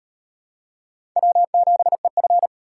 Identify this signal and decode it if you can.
Morse code, heard as a keyed tone, about 36 wpm, 700 Hz. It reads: W7EF